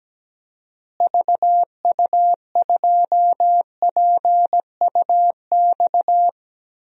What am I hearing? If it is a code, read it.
Morse code, heard as a keyed tone, about 17 wpm, 700 Hz. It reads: VU2PUX
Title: DXCC-100 Call Signs - 6 Characters - Repeated 3x - 17wpm